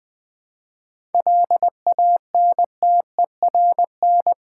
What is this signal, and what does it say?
Morse code, heard as a keyed tone, about 20 wpm, 700 Hz. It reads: LANTERN
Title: Easy Words with Hints - 20wpm